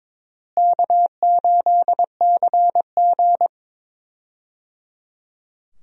Morse code, heard as a keyed tone, about 22 wpm, 700 Hz. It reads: K8CG
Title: USA Call Signs - 4 Characters - Repeated 3x - 22wpm